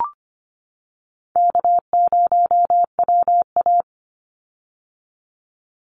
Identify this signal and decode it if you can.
Morse code, heard as a keyed tone, about 25 wpm, 700 Hz. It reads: K0WA
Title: Call Signs - 25wpm